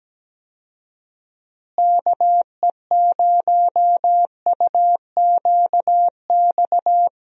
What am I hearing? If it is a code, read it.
Morse code, heard as a keyed tone, about 17 wpm, 700 Hz. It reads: KE0UQX
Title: USA Call Signs - 6 Characters - Repeated 3x - 17wpm